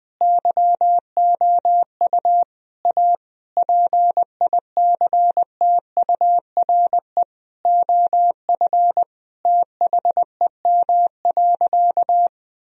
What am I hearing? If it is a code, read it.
Morse code, heard as a keyed tone, about 20 wpm, 700 Hz. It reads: YOU A PICTURE OF THEM.
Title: Sentences from Top 1000 Words - Review Part 1 - 20wpm